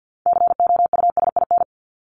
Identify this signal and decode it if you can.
Morse code, heard as a keyed tone, about 50 wpm, 700 Hz. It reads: COUSIN
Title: Top 1501-1750 Words 50wpm